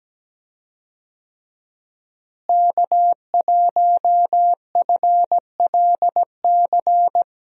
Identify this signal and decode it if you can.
Morse code, heard as a keyed tone, about 17 wpm, 700 Hz. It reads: K1FLC
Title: USA Call Signs - 5 Characters - Repeated 2x - 17wpm